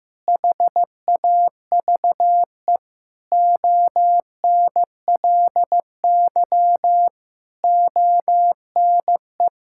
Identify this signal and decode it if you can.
Morse code, heard as a keyed tone, about 16 wpm, 700 Hz. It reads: HAVE ONLY ONE
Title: Sentences from Top 1000 Words - Review Part 6 - 15wpm